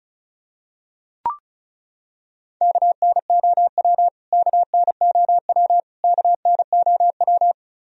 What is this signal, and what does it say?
Morse code, heard as a keyed tone, about 34 wpm, 700 Hz. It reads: KNOW KNOW KNOW
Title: Top 100 Words Repeated 35wpm